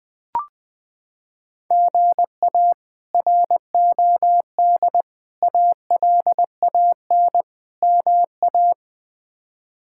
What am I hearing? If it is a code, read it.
Morse code, heard as a keyed tone, about 20 wpm, 700 Hz. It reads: GA ROD ALAN MA
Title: K1USN SST QSO Practice - 20wpm